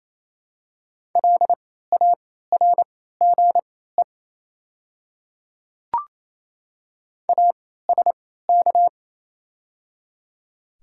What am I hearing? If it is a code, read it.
Morse code, heard as a keyed tone, about 28 wpm, spaced as for 16 wpm, 700 Hz. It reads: LARGE ASK
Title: 3x Character Spacing - Top 200 Words - 28wpm